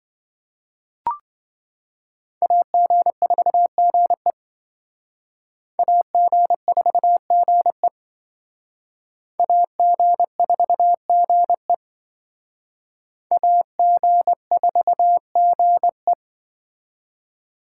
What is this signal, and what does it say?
Morse code, heard as a keyed tone, about 26 wpm, 700 Hz. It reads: AG4GE AG4GE AG4GE AG4GE